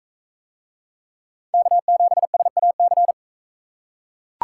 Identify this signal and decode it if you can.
Morse code, heard as a keyed tone, about 39 wpm, 700 Hz. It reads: K7SAC